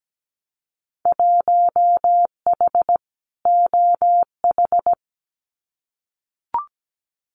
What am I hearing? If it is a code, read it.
Morse code, heard as a keyed tone, about 17 wpm, 700 Hz. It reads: 1H OH
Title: Winter Field Day Exchanges 17wpm